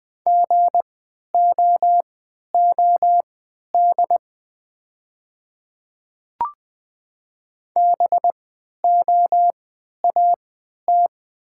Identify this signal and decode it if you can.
Morse code, heard as a keyed tone, about 20 wpm, spaced as for 11 wpm, 690 Hz. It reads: GOOD BOAT
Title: 3x Character Spacing - Top 500 Words - 20wpm